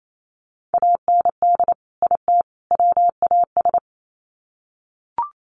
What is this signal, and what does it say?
Morse code, heard as a keyed tone, about 28 wpm, 700 Hz. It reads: AND IT WAS
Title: Sets of 3 Words 28wpm